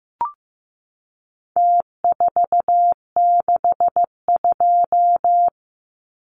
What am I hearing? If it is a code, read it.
Morse code, heard as a keyed tone, about 15 wpm, 700 Hz. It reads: T462
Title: Numbers - Contest Style - 4-Digit 15wpm